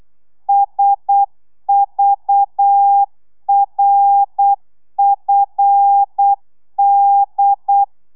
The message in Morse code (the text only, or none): SVRFD